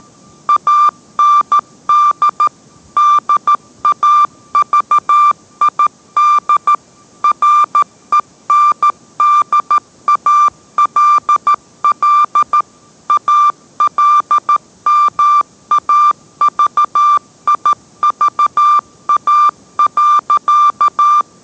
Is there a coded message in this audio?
Yes: Morse code from 0.5 to 21.2 s, heard as a keyed tone, about 13 wpm, 1180 Hz. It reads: AND DAVID RENDALL ALMAVIVA.